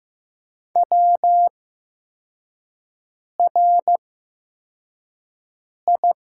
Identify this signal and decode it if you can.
Morse code, heard as a keyed tone, about 15 wpm, spaced as for 4 wpm, 700 Hz. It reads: WRI
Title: 8x Character Spacing - Top 500 Words - 15wpm